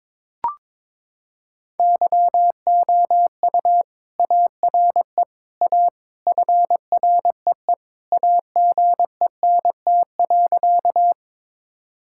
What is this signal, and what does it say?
Morse code, heard as a keyed tone, about 21 wpm, 700 Hz. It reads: YOU ARE A FREE AGENT.
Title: Sentences from Top 1600 Words 22wpm